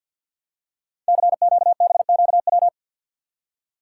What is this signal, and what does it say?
Morse code, heard as a keyed tone, about 47 wpm, 700 Hz. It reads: CQBQW
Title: Code Groups - 5 Letter:Numbers - 50wpm